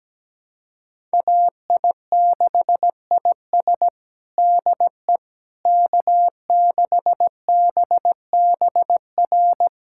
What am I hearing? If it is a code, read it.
Morse code, heard as a keyed tone, about 17 wpm, 700 Hz. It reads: AI6IS DE K6BBR
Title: USA Callsign DE Callsign - 5 Characters - 17wpm